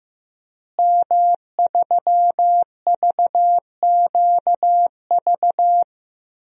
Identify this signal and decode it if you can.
Morse code, heard as a keyed tone, about 15 wpm, 700 Hz. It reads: M3VQV
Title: DXCC-100 Call Signs - 5 Characters - Repeated 3x - 15wpm